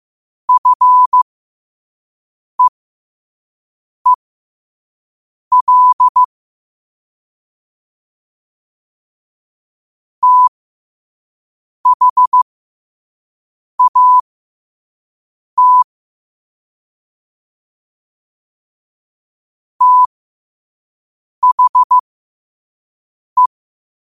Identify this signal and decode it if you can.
Morse code, heard as a keyed tone, about 15 wpm, spaced as for 5 wpm, 1000 Hz. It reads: FEEL THAT THE
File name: qotdfiles / 5wpm